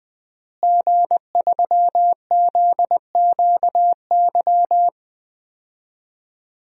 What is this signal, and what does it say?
Morse code, heard as a keyed tone, about 19 wpm, 700 Hz. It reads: G3ZQY
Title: DXCC-100 Call Signs - 5 Characters - Repeated 3x - 20wpm